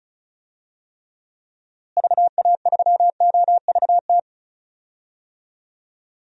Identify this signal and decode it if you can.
Morse code, heard as a keyed tone, about 36 wpm, 700 Hz. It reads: VA3OVT
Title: DXCC-100 Call Signs - 6 Characters - Repeated 3x - 35wpm